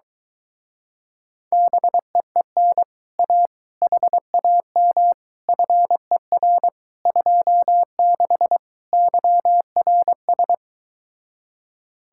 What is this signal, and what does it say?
Morse code, heard as a keyed tone, about 24 wpm, 700 Hz. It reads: BEEN A HAM FER 26 YRS